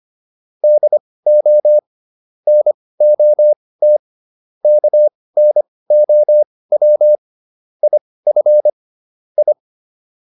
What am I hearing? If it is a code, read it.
Morse code, heard as a keyed tone, about 25 wpm, spaced as for 18 wpm, 600 Hz. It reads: DO NOT KNOW IF I